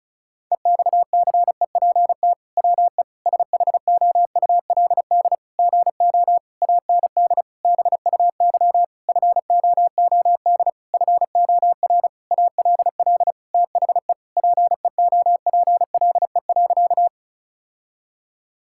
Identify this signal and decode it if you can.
Morse code, heard as a keyed tone, about 34 wpm, 700 Hz. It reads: EXCEPT WE SHOULD GO AND BUY FOOD FOR ALL THE PEOPLE.
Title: Sentences from Top 1600 Words 35wpm